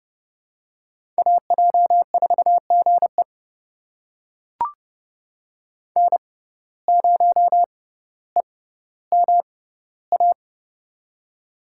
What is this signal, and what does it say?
Morse code, heard as a keyed tone, about 30 wpm, spaced as for 15 wpm, 700 Hz. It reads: AJ4GE N0EMA
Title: USA Call Signs - 5 Characters - 6x Character Spacing - 30wpm